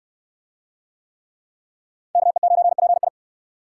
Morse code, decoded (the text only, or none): KE9PI